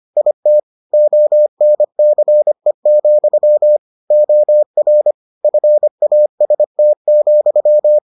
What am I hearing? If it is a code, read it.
Morse code, heard as a keyed tone, about 25 wpm, 600 Hz. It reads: IT ONCE, OR FAST,